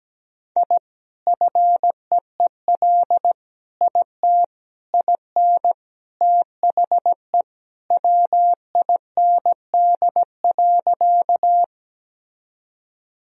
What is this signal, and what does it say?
Morse code, heard as a keyed tone, about 17 wpm, 700 Hz. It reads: I FEEL IT IN THE WIND.
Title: Sentences from Top 1000 Words - Review Part 2 - 17wpm